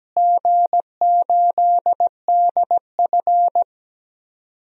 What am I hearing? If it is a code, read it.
Morse code, heard as a keyed tone, about 17 wpm, 700 Hz. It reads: G8DF
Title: DXCC-100 Call Signs - 4 Characters - Repeated 3x - 17wpm